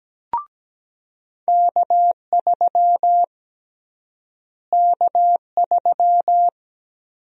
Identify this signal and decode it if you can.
Morse code, heard as a keyed tone, about 17 wpm, 700 Hz. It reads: K3 K3
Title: USA Call Sign Prefixes - Repeated 2x - 17wpm